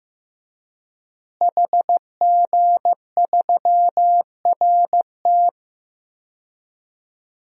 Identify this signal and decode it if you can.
Morse code, heard as a keyed tone, about 15 wpm, 700 Hz. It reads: HG3RT
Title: DXCC-100 Call Signs - 5 Characters - Repeated 3x - 15wpm